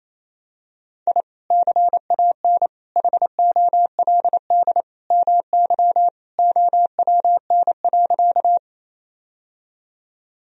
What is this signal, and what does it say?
Morse code, heard as a keyed tone, about 27 wpm, 700 Hz. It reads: I CAN HOLD MY OWN.
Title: Sentences from Top 1000 Words - Review Part 2 - 28wpm